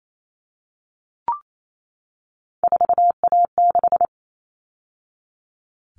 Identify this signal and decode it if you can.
Morse code, heard as a keyed tone, about 28 wpm, 700 Hz. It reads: 4A6
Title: All Cut Numbers - 3-Digit 28wpm